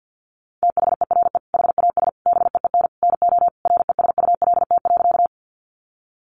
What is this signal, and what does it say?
Morse code, heard as a keyed tone, about 50 wpm, 700 Hz. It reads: THERE HAS BEEN NO RESULT.
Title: Sentences from Top 700 Words 50wpm